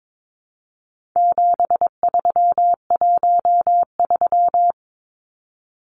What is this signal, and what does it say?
Morse code, heard as a keyed tone, about 22 wpm, 700 Hz. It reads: 7313
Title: Numbers - 4-Digit 22wpm